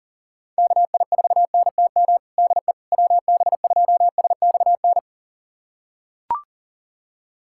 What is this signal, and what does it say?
Morse code, heard as a keyed tone, about 37 wpm, 700 Hz. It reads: KI4NTM DE WB2SXN
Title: USA Callsign DE Callsign 40wpm